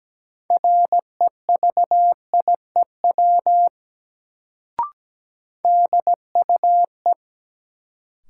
Morse code, heard as a keyed tone, about 17 wpm, 700 Hz. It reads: REVIEW DUE
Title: Sentences from Top 900 Words - New Word Drill - 17wpm